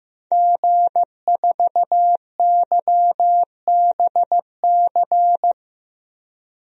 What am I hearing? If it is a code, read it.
Morse code, heard as a keyed tone, about 15 wpm, 700 Hz. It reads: G4YBC
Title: DXCC-100 Call Signs - 5 Characters - Repeated 2x - 15wpm